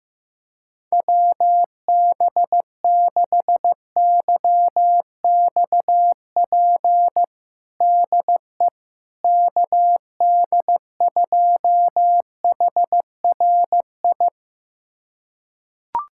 Morse code, heard as a keyed tone, about 15 wpm, 700 Hz. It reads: WB6YXP DE KD2HRI
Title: USA Callsign DE Callsign 15wpm